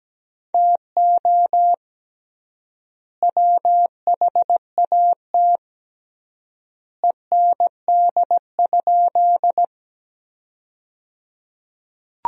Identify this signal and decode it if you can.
Morse code, heard as a keyed tone, about 17 wpm, 700 Hz. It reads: TO WHAT END?
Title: 3x Word-Spacing - Sentences from Top 200 Words - 17wpm